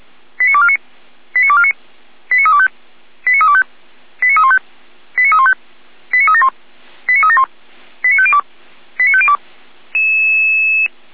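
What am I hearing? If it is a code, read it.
Five-tone selective call (ZVEI), heard as quick runs of five tones, tones of about 70 ms. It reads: 89249 89249 89325 89325 89315 89315 89371 89371 89502 89502
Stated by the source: Received via SDR as a ZVEI five-tone call, address 89337, on the Feuerwehr Orlamünde fire brigade audio server.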